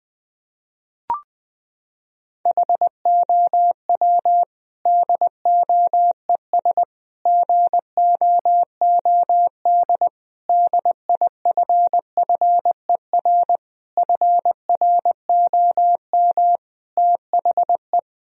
Morse code, heard as a keyed tone, about 20 wpm, 700 Hz. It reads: HOW DOES GOOD DIFFER FROM THE
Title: Sentences from Top 1000 Words - Review Part 1 - 20wpm